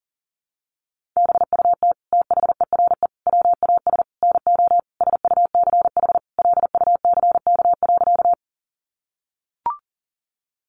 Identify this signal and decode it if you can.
Morse code, heard as a keyed tone, about 40 wpm, 700 Hz. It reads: BUT THERE WAS NO SUCH LUCK.
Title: Sentences from Top 1500 Words 40wpm